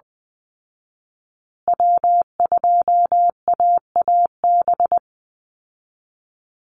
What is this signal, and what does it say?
Morse code, heard as a keyed tone, about 20 wpm, 700 Hz. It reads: W2AAB
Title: Call Signs Repeated 2x - 20wpm